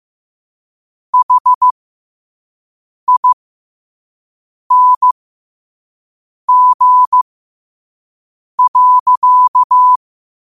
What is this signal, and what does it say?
Morse code, heard as a keyed tone, about 15 wpm, spaced as for 5 wpm, 1000 Hz. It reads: HING.
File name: qotdfiles / 5wpm